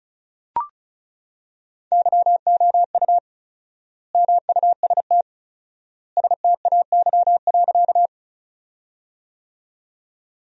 Morse code, heard as a keyed tone, about 36 wpm, 700 Hz. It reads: YOU MUST STAY.
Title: 4x Word-Spacing - Sentences from Top 500 Words - 35wpm